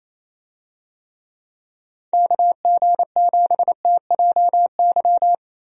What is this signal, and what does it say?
Morse code, heard as a keyed tone, about 28 wpm, 700 Hz. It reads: KG7TJY
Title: USA Call Signs - 6 Characters - Repeated 3x - 28wpm